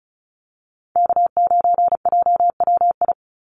Morse code, heard as a keyed tone, about 34 wpm, 700 Hz. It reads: K9JWI
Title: Call Signs Repeated 2x - 35wpm